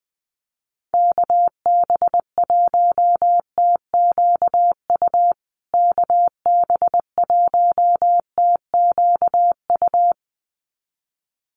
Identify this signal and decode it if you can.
Morse code, heard as a keyed tone, about 20 wpm, 700 Hz. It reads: KB1TQU KB1TQU
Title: USA Call Signs Repeated 2x 20wpm